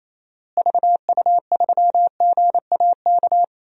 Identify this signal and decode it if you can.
Morse code, heard as a keyed tone, about 28 wpm, 700 Hz. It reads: VU3GAK
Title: DXCC-100 Call Signs - 6 Characters - Repeated 2x - 28wpm